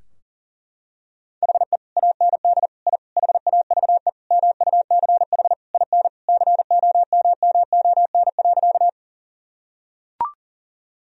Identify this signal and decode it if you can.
Morse code, heard as a keyed tone, about 40 wpm, 700 Hz. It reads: HE AND I HAVE MUCH IN COMMON.
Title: Sentences from Top 1000 Words - Review Part 3 - 40wpm